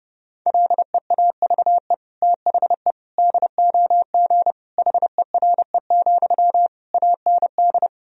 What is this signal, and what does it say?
Morse code, heard as a keyed tone, about 30 wpm, 700 Hz. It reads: LEAVE THE DOG HERE, AND
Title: Sentences from Top 1000 Words - Review Part 2 - 30wpm